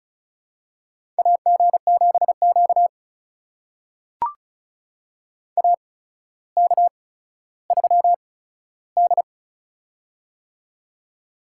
Morse code, heard as a keyed tone, about 36 wpm, spaced as for 15 wpm, 700 Hz. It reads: AG7Q AK3D
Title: USA Call Signs - 4 Characters - 8x Character Spacing - 35wpm